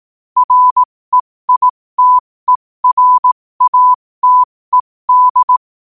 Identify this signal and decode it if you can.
Morse code, heard as a keyed tone, about 18 wpm, spaced as for 15 wpm, 1000 Hz. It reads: REITERATED
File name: qotdfiles / 15wpm